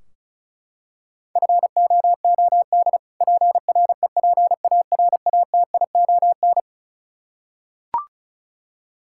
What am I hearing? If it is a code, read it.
Morse code, heard as a keyed tone, about 36 wpm, 700 Hz. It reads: FOOD PREPARATION
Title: Practice QSO - Exchange 3 - More Info - 35wpm